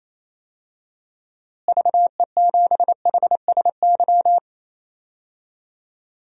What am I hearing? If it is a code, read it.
Morse code, heard as a keyed tone, about 28 wpm, 700 Hz. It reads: VE7HSY
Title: DXCC-100 Call Signs - 6 Characters - Repeated 2x - 28wpm